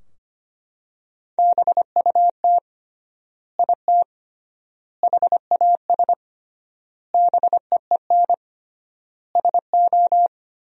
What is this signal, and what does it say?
Morse code, heard as a keyed tone, about 25 wpm, 700 Hz. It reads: BUT IT HAS BEEN SO
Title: 3x Word-Spacing - Sentences from Top 100 Words - 25wpm